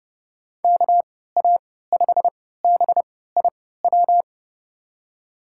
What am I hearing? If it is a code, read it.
Morse code, heard as a keyed tone, about 30 wpm, spaced as for 17 wpm, 700 Hz. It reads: KA5BIW
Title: USA Call Signs - 6 Characters - 3x Character Spacing - 30wpm